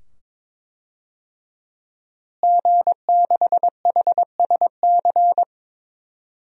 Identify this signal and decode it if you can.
Morse code, heard as a keyed tone, about 21 wpm, 700 Hz. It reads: G6HSC
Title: DXCC-100 Call Signs - 5 Characters - Repeated 2x - 22wpm